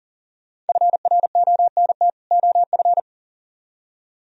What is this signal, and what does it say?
Morse code, heard as a keyed tone, about 49 wpm, 700 Hz. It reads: FRONT OF
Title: Speed-Racing - Sets of 2 Words 40wpm